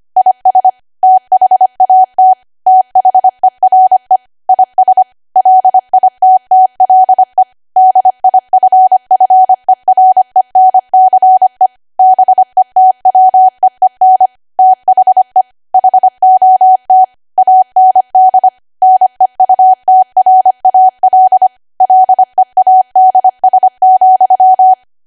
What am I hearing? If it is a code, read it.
Morse code, heard as a keyed tone, about 25 wpm, 750 Hz. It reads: IS THAT THERE IS LITTLE DIFFERENCE BETWEEN THE HOT AND NEUTRAL LEADS,
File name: cwa / W1AW-Level3 / 25WPM